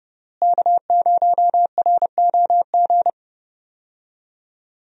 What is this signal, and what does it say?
Morse code, heard as a keyed tone, about 30 wpm, 700 Hz. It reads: K0ROG
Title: USA Call Signs - 5 Characters - Repeated 3x - 30wpm